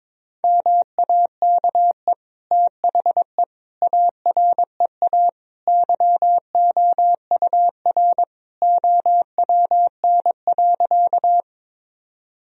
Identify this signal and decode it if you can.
Morse code, heard as a keyed tone, about 21 wpm, 700 Hz. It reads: MAKE THE AREA YOUR OWN.
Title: Sentences from Top 1000 Words - Review Part 2 - 22wpm